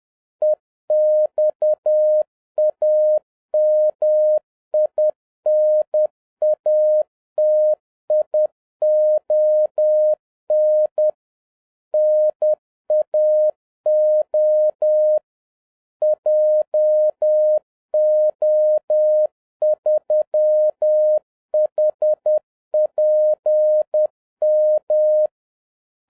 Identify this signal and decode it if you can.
Morse code, heard as a keyed tone, about 10 wpm, 610 Hz. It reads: EXAMINATION NAO JO3HPM